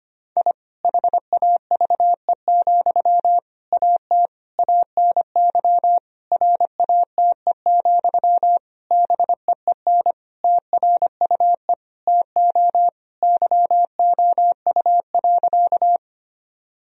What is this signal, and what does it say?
Morse code, heard as a keyed tone, about 25 wpm, 700 Hz. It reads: I HAVE, AT ANY RATE, BEEN TRUE TO YOU.
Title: Sentences from Top 1000 Words - Review Part 4 - 25wpm